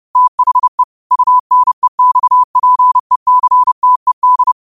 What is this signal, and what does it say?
Morse code, heard as a keyed tone, about 30 wpm, 1000 Hz. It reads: THE UNEXPECTED
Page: Quotes for Mon, 28 Apr 2025 in Morse Code at 30 words per minute.